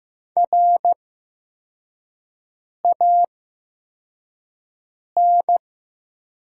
Morse code, heard as a keyed tone, about 15 wpm, spaced as for 4 wpm, 700 Hz. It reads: RAN
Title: 8x Character Spacing - Top 500 Words - 15wpm